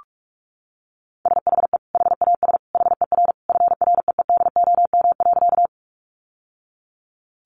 Morse code, heard as a keyed tone, about 44 wpm, 700 Hz. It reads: SHE HAS HER FREEDOM.
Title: Sentences from Top 800 Words 45wpm